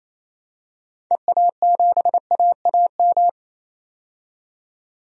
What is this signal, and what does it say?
Morse code, heard as a keyed tone, about 28 wpm, 700 Hz. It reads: EA7AAM
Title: DXCC-100 Call Signs - 6 Characters - 28wpm